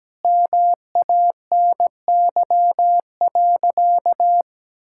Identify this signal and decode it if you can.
Morse code, heard as a keyed tone, about 17 wpm, 700 Hz. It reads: MANY.
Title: Sentences from Top 1000 Words - Review Part 6 - 17wpm